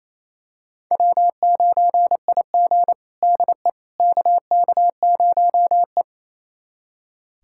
Morse code, heard as a keyed tone, about 28 wpm, 700 Hz. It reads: W9IG DE KK0E
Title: USA Callsign DE Callsign - 4 Characters - 28wpm